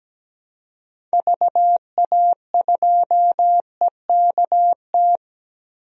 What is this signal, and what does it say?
Morse code, heard as a keyed tone, about 17 wpm, 700 Hz. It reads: VA2EKT
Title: DXCC-100 Call Signs - 6 Characters - Repeated 2x - 17wpm